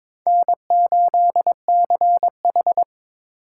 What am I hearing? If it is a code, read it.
Morse code, heard as a keyed tone, about 22 wpm, 700 Hz. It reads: N8CH
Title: USA Call Signs - 3x Character Spacing - 22wpm